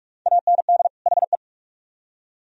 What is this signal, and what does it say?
Morse code, heard as a keyed tone, about 54 wpm, 700 Hz. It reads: AND HE
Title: Speed-Racing - Sets of 2 Words 45wpm